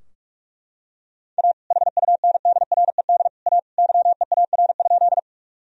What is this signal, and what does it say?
Morse code, heard as a keyed tone, about 43 wpm, 700 Hz. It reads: A HUNDRED A YEAR?
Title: Sentences from Top 1000 Words - Review Part 2 - 45wpm